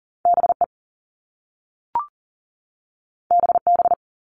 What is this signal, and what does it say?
Morse code, heard as a keyed tone, about 40 wpm, 700 Hz. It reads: BE BB
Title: All Cut Numbers - 2-Digit 40wpm